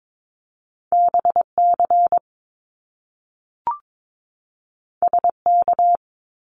Morse code, heard as a keyed tone, about 22 wpm, 700 Hz. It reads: BC SK
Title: Canadian Provinces and Territories 22wpm